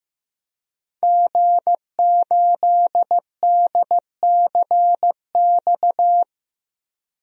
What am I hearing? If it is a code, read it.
Morse code, heard as a keyed tone, about 16 wpm, 700 Hz. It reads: G8DCX
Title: DXCC-100 Call Signs - 5 Characters - 15wpm